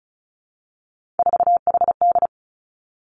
Morse code, heard as a keyed tone, about 36 wpm, 700 Hz. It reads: 4HD
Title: Three Challenging Letter-Number Combinations 35wpm